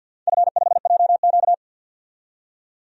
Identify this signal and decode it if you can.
Morse code, heard as a keyed tone, about 50 wpm, 700 Hz. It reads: F5JQ
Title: DXCC-100 Call Signs - 4 Characters - 50wpm